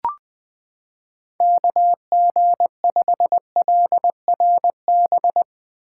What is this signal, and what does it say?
Morse code, heard as a keyed tone, about 20 wpm, 700 Hz. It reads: KG5LRB
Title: 2023 POTA Activator Call Signs - 20wpm